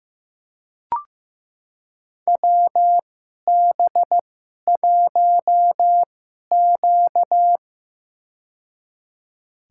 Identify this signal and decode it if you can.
Morse code, heard as a keyed tone, about 15 wpm, spaced as for 11 wpm, 700 Hz. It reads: WB1Q
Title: USA Call Signs - 4 Characters - 2x Character Spacing - 15wpm